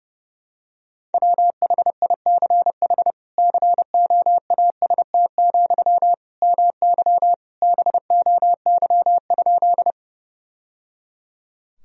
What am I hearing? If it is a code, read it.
Morse code, heard as a keyed tone, about 30 wpm, 700 Hz. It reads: WHICH COAST, MY BOY?
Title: Sentences from Top 1000 Words - Review Part 4 - 30wpm